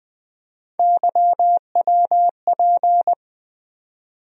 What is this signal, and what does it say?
Morse code, heard as a keyed tone, about 20 wpm, 700 Hz. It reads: YWP
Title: USA Call Sign Postfixes - 3 Letters 20wpm